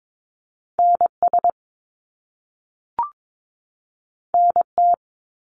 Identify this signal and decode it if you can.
Morse code, heard as a keyed tone, about 22 wpm, 700 Hz. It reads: NS NT
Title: Canadian Provinces and Territories 22wpm